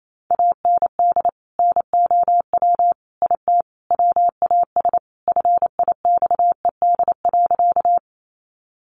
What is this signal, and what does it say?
Morse code, heard as a keyed tone, about 28 wpm, 700 Hz. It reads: AND NOW IT WAS FIXED.
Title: Sentences from Top 1100 Words 28wpm